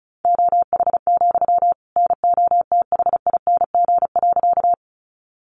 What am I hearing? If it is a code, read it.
Morse code, heard as a keyed tone, about 34 wpm, 700 Hz. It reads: OH, NOTHING.
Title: Sentences from Top 500 Words 35wpm